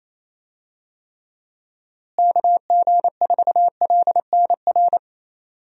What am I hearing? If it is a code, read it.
Morse code, heard as a keyed tone, about 28 wpm, 700 Hz. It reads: KG4LNR
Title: USA Call Signs - 6 Characters - Repeated 3x - 28wpm